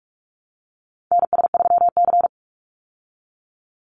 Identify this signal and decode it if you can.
Morse code, heard as a keyed tone, about 44 wpm, 700 Hz. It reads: NS3C
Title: Call Signs Repeated 2x - 45wpm